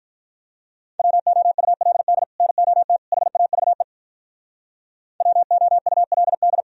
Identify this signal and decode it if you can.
Morse code, heard as a keyed tone, about 46 wpm, 700 Hz. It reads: WOULD NOT HAVE WOULD